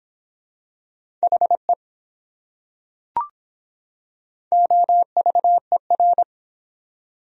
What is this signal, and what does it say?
Morse code, heard as a keyed tone, about 24 wpm, 700 Hz. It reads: HE OVER